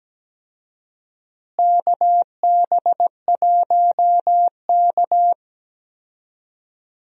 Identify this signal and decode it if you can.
Morse code, heard as a keyed tone, about 17 wpm, 700 Hz. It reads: KB1K
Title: USA Call Signs - 4 Characters - Repeated 2x - 17wpm